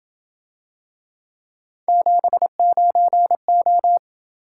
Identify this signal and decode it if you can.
Morse code, heard as a keyed tone, about 28 wpm, 700 Hz. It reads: 79O